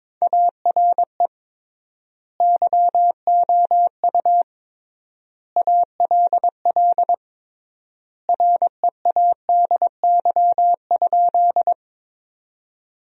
Morse code, heard as a keyed tone, about 22 wpm, 700 Hz. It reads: ARE YOU ALL READY?
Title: 3x Word-Spacing - Sentences from Top 400 Words - 22wpm